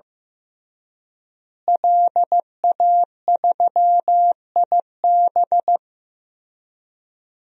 Morse code, heard as a keyed tone, about 15 wpm, 700 Hz. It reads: LA3IB
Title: DXCC-100 Call Signs - 5 Characters - Repeated 3x - 15wpm